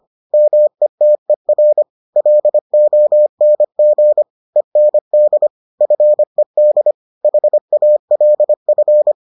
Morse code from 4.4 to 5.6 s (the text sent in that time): END